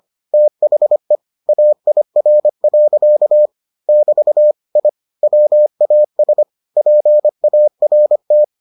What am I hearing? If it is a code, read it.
Morse code, heard as a keyed tone, about 25 wpm, 600 Hz. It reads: THE AIR. <BT> I WAS PART